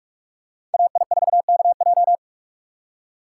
Morse code, heard as a keyed tone, about 46 wpm, 700 Hz. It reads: AI4KJ
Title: USA Call Signs - 5 Characters - 3x Character Spacing - 45wpm